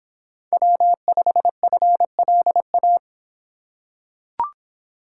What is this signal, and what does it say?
Morse code, heard as a keyed tone, about 25 wpm, 700 Hz. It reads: W5FLA